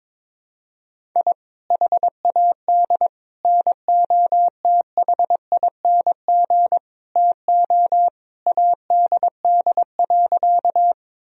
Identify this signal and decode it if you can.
Morse code, heard as a keyed tone, about 22 wpm, 700 Hz. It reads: I HAD NOTHING TO ADD.
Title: Sentences from Top 1000 Words - Review Part 1 - 22wpm